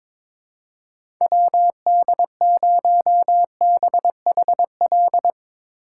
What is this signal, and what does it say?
Morse code, heard as a keyed tone, about 22 wpm, 700 Hz. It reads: WD0BHL
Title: USA Call Signs - 6 Characters - 8x Character Spacing - 22wpm